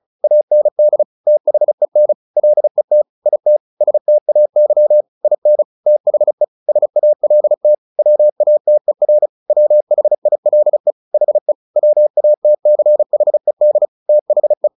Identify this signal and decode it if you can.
Morse code, heard as a keyed tone, about 34 wpm, 600 Hz. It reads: AND THEN LET IT STAY IN THE SALT WATER WHILE HE WATCHED THE
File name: the old man and the sea 2 35wpm